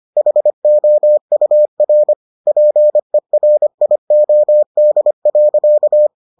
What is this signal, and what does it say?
Morse code, heard as a keyed tone, about 25 wpm, 600 Hz. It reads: HOUR PERIOD.